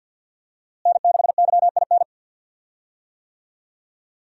Morse code, heard as a keyed tone, about 52 wpm, 700 Hz. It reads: N6YIN
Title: USA Call Signs - 5 Characters - Repeated 2x - 50wpm